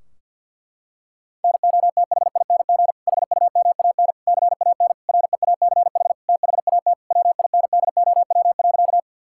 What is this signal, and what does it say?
Morse code, heard as a keyed tone, about 50 wpm, 700 Hz. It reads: NOTHING HUMAN CAN REACH THAT WINDOW.
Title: Sentences from Top 1600 Words 50wpm